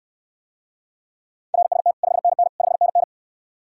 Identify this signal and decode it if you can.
Morse code, heard as a keyed tone, about 69 wpm, 700 Hz. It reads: RST 5NN 5NN